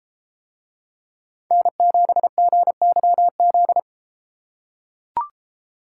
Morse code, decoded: N7GYZ

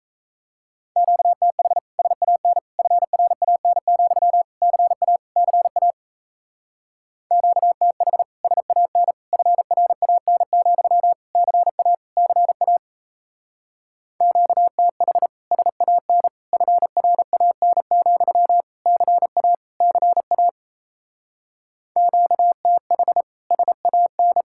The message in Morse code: QTH SAN FRAN, CA CA QTH SAN FRAN, CA CA QTH SAN FRAN, CA CA QTH SAN

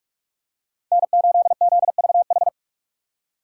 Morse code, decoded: N8ZVH

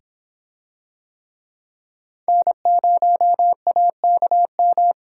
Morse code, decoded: N0AKM

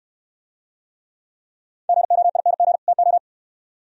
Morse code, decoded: GOING TO

